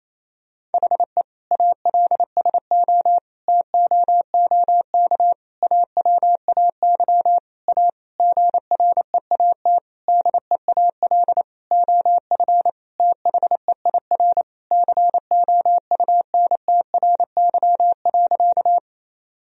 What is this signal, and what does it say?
Morse code, heard as a keyed tone, about 28 wpm, 700 Hz. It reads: HE ALSO TOOK AWAY A GREAT DEAL OF THEIR COUNTRY.
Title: Sentences from Top 1000 Words - Review Part 1 - 28wpm